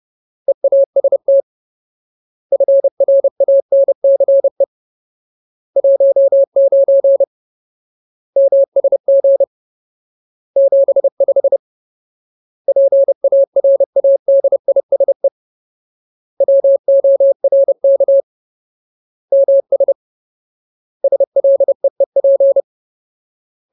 Morse code, heard as a keyed tone, about 30 wpm, 550 Hz. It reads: EAST FRANCE 19 MSG 75 PARADISE WORK MS SLEEP
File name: Ultimate Word List / 30WPM Ultimate